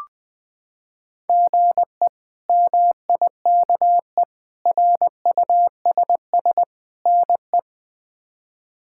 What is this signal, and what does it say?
Morse code, heard as a keyed tone, about 21 wpm, 700 Hz. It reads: GE MIKE RUSS NE
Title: K1USN SST QSO Practice - 20wpm